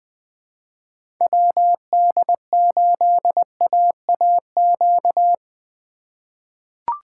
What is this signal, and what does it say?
Morse code, heard as a keyed tone, about 21 wpm, 700 Hz. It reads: WD8AAQ
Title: USA Call Signs - 6 Characters - 4x Character Spacing - 20wpm